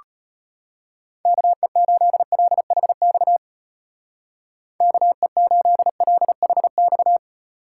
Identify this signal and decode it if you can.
Morse code, heard as a keyed tone, about 32 wpm, 700 Hz. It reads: KE8LHX KE8LHX